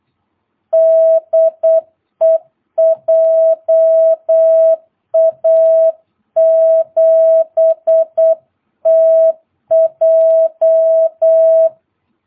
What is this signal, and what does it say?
Morse code, heard as a keyed tone, about 8 wpm, 660 Hz. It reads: DE JA7TJ
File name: jh8cbh / 2022cw